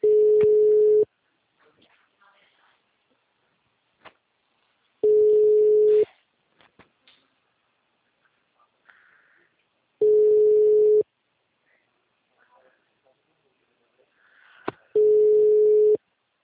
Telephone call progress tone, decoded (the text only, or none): ringback tone